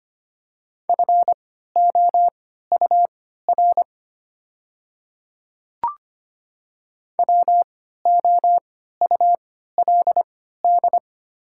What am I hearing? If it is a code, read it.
Morse code, heard as a keyed tone, about 25 wpm, spaced as for 14 wpm, 700 Hz. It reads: FOUR WOULD
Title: 3x Character Spacing - Top 300 Words - 25wpm